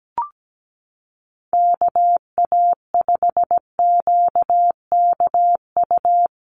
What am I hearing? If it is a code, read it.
Morse code, heard as a keyed tone, about 17 wpm, 700 Hz. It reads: KA5QKU
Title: USA Call Signs Repeated 2x 17wpm